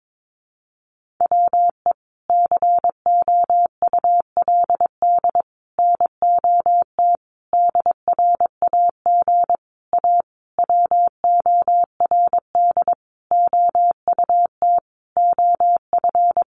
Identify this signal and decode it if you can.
Morse code, heard as a keyed tone, about 22 wpm, 700 Hz. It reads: WE COULD NOT DRAG A WORD OUT OF
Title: Sentences from Top 1500 Words 22wpm